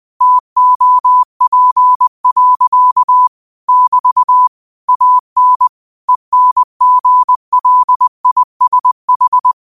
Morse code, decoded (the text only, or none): TOP. <BT> AN ENGLISH